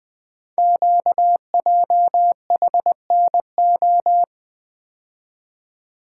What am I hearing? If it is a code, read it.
Morse code, heard as a keyed tone, about 20 wpm, 700 Hz. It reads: QJHNO
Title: Code Groups - 5 Letters - 20wpm